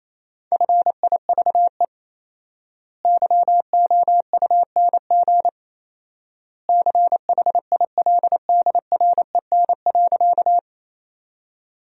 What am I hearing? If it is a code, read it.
Morse code, heard as a keyed tone, about 28 wpm, 700 Hz. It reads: FIVE YOUNG CHILDREN.
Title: 4x Word-Spacing - Sentences from Top 400 Words - 28wpm